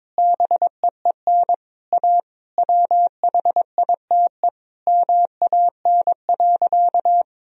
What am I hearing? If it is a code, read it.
Morse code, heard as a keyed tone, about 22 wpm, 700 Hz. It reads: BEEN A WHITE MAN.
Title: Sentences from Top 1000 Words - Review Part 1 - 22wpm